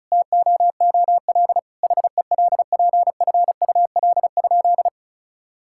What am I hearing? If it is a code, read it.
Morse code, heard as a keyed tone, about 34 wpm, 700 Hz. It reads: TOOL HELPFUL?
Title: Sentences from Top 1000 Words - Review Part 6 - 35wpm